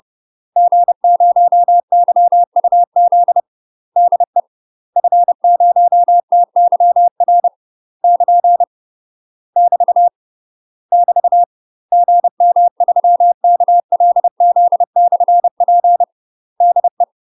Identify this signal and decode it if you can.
Morse code, heard as a keyed tone, about 30 wpm, 700 Hz. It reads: G0YUZ DE F0TYR <KN> <BT> <BT> GM3KLZ/P DE
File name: UK MASTER QSOs TRAINING / 30WPM / UK MASTER QSOs 0030